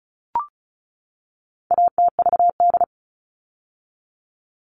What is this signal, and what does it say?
Morse code, heard as a keyed tone, about 34 wpm, 700 Hz. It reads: ATVD
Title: All Cut Numbers - 4-Digit 35wpm